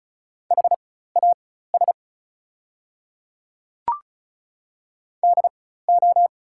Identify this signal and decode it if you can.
Morse code, heard as a keyed tone, about 36 wpm, spaced as for 16 wpm, 690 Hz. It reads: HAS DO
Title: 4x Character Spacing - Top 100 Words - 35wpm